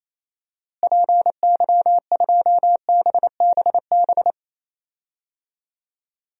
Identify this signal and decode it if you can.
Morse code, heard as a keyed tone, about 28 wpm, 700 Hz. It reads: PY2BBB
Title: DXCC-100 Call Signs - 6 Characters - Repeated 2x - 28wpm